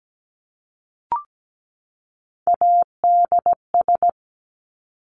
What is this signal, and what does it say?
Morse code, heard as a keyed tone, about 17 wpm, 700 Hz. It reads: ADS
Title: Abbreviations 17wpm